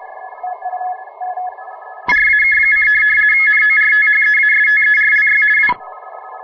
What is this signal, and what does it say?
RTTY, about 50 baud. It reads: RAK: NORTHEASTERLY W
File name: fr / DX RTTY